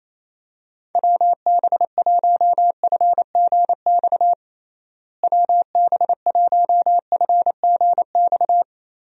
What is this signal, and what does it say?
Morse code, heard as a keyed tone, about 27 wpm, 700 Hz. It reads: WB1FGX WB1FGX
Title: USA Call Signs - 6 Characters - Repeated 3x - 28wpm